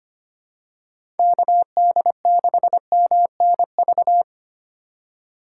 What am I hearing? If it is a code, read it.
Morse code, heard as a keyed tone, about 25 wpm, 700 Hz. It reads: KD6MNV
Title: USA Call Signs - 6 Characters - 2x Character Spacing - 25wpm